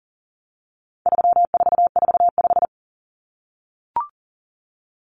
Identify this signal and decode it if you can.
Morse code, heard as a keyed tone, about 40 wpm, 700 Hz. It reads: 3445
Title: Numbers - 4-Digit 40wpm